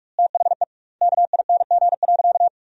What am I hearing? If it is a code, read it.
Morse code, heard as a keyed tone, about 46 wpm, 700 Hz. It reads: THE KING.
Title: 2x Word-Spacing - Sentences from Top 400 Words - 45wpm